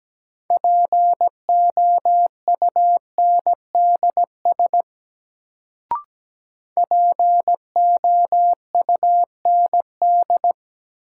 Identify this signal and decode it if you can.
Morse code, heard as a keyed tone, about 17 wpm, 700 Hz. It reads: POUNDS POUND
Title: Sentences from Top 1000 Words - Review Part 2 - New Word Drill - 17wpm